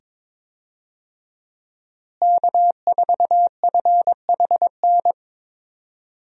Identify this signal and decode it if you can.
Morse code, heard as a keyed tone, about 22 wpm, 700 Hz. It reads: K4FHN
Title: USA Call Signs - 5 Characters - Repeated 2x - 22wpm